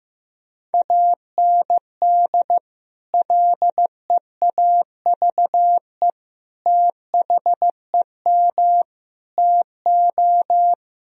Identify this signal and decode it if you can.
Morse code, heard as a keyed tone, about 15 wpm, 700 Hz. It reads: AND LEAVE THEM TO
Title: Sentences from Top 1000 Words - Review Part 2 - 15wpm